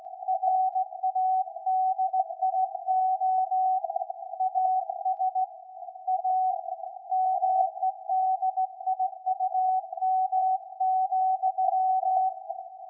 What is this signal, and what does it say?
Morse code, heard as a keyed tone, about 17 wpm, 750 Hz. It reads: RADIO AS A MEDIUM,
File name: iz7slz / VLF / SAQ 13-02-2023